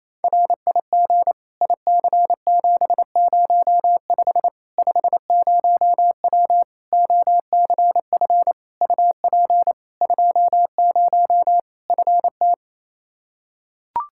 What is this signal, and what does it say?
Morse code, heard as a keyed tone, about 28 wpm, 700 Hz. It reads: RIG IC705 50W OCF UP 20 FT
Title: Practice QSO - Exchange 2 - Info - 28wpm